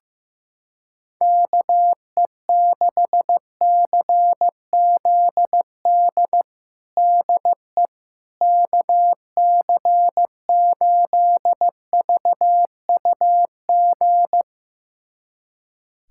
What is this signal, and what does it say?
Morse code, heard as a keyed tone, about 15 wpm, 700 Hz. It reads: KE6CZD DE KC8VUG
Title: USA Callsign DE Callsign - 6 Characters - 15wpm